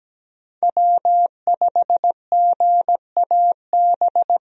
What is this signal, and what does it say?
Morse code, heard as a keyed tone, about 17 wpm, 700 Hz. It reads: W5GAB
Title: USA Call Signs - 5 Characters - Repeated 2x - 17wpm